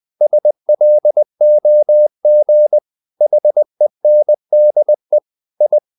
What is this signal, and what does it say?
Morse code, heard as a keyed tone, about 20 wpm, 600 Hz. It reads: SLOG HENDE I
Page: Ferie 20 wpm